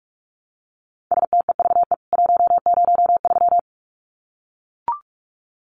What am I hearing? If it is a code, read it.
Morse code, heard as a keyed tone, about 44 wpm, 700 Hz. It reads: STEVE 103
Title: CWT Exchange 45wpm